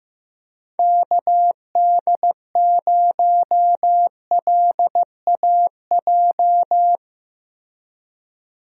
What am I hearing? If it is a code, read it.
Morse code, heard as a keyed tone, about 15 wpm, 700 Hz. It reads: KD0LAJ
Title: USA Call Signs - 6 Characters - 15wpm